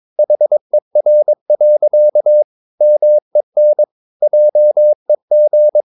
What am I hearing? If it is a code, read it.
Morse code, heard as a keyed tone, about 22 wpm, 600 Hz. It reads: HER. MEN JEG
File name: Ferie QRS 22wpm